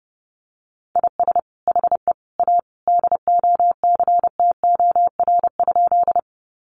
Morse code, heard as a keyed tone, about 30 wpm, 700 Hz. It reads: IS HE A DOCTOR?
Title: Sentences from Top 1500 Words 30wpm